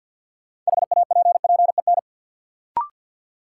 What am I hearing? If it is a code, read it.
Morse code, heard as a keyed tone, about 53 wpm, 700 Hz. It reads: HAPPEN
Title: Sentences from Top 400 Words - New Word Drill - 50wpm